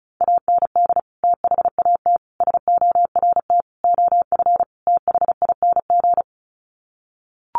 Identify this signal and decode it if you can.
Morse code, heard as a keyed tone, about 34 wpm, 700 Hz. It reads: AND THAT SORT OF THING
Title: Sets of 5 Words Encore2 35wpm